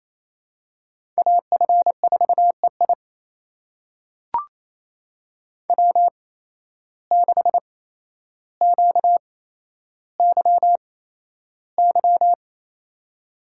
Morse code, heard as a keyed tone, about 27 wpm, spaced as for 12 wpm, 700 Hz. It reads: AF4EI W6QYY
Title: USA Call Signs - 5 Characters - 8x Character Spacing - 28wpm